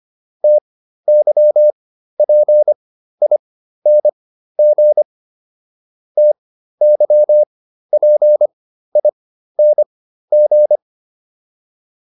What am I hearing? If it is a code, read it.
Morse code, heard as a keyed tone, about 25 wpm, spaced as for 13 wpm, 600 Hz. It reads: TYPING TYPING